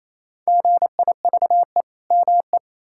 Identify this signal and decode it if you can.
Morse code, heard as a keyed tone, about 29 wpm, 700 Hz. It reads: GIVE ME
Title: Sentences from Top 1000 Words - Review Part 5 - 28wpm